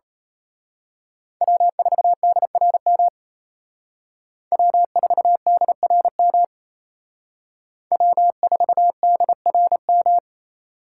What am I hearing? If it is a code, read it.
Morse code, heard as a keyed tone, about 31 wpm, 700 Hz. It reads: W4DRM W4DRM W4DRM
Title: Speed Racing - USA 1x3 Call Signs 28wpm